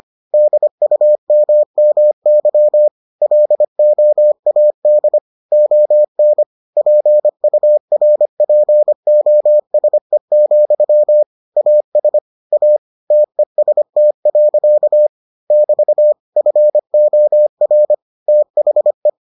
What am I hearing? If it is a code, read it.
Morse code, heard as a keyed tone, about 25 wpm, 600 Hz. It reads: DUMMY LOAD ON PURPOSE, AS A TEST. <BT> FOR THE